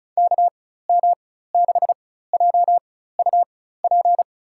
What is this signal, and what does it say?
Morse code, heard as a keyed tone, about 34 wpm, spaced as for 16 wpm, 700 Hz. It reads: KM6JUP
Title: USA Call Signs - 6 Characters - 4x Character Spacing - 35wpm